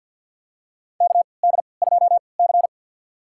Morse code, heard as a keyed tone, about 51 wpm, spaced as for 28 wpm, 700 Hz. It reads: KD2C
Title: USA Call Signs - 4 Characters - 3x Character Spacing - 50wpm